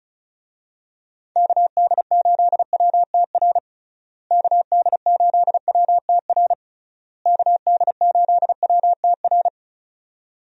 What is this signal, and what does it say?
Morse code, heard as a keyed tone, about 34 wpm, 700 Hz. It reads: KD8WTR KD8WTR KD8WTR
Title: USA Call Signs - 6 Characters - Repeated 3x - 35wpm